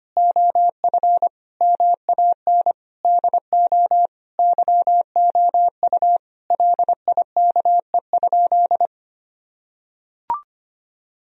Morse code, heard as a keyed tone, about 25 wpm, 700 Hz. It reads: OF MAN DO YOU LIKE?
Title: Sentences from Top 1000 Words - Review Part 3 - 25wpm